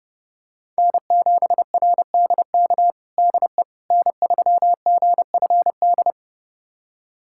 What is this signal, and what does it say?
Morse code, heard as a keyed tone, about 32 wpm, 700 Hz. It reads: N7RDK DE N3GFD
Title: USA Callsign DE Callsign - 5 Characters - 30wpm